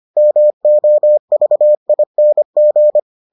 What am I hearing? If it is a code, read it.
Morse code, heard as a keyed tone, about 25 wpm, 600 Hz. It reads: MOVING